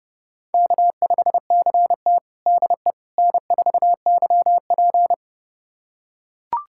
Morse code, heard as a keyed tone, about 30 wpm, 700 Hz. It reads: K5CT DE N4YP
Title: USA Callsign DE Callsign - 4 Characters - 30wpm